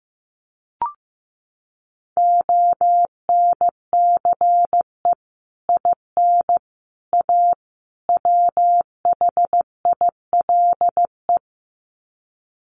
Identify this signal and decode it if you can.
Morse code, heard as a keyed tone, about 15 wpm, 700 Hz. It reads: ONCE IN A WHILE
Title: Sets of 4 Words 15wpm